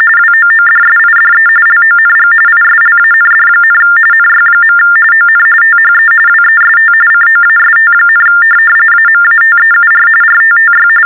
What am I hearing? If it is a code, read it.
TTY/TDD text telephone tones, about 45.45 baud. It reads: WOODSIDE LONG ISLAND, NY. ON THE WEST COAST, AMATEUR RTTY
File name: teletype-RTTY-45